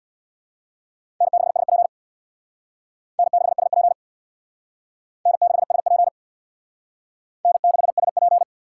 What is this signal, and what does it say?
Morse code, heard as a keyed tone, about 61 wpm, 700 Hz. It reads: N6SP N6SP N6SP N6SP
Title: Speed Racing - USA Call Signs 50wpm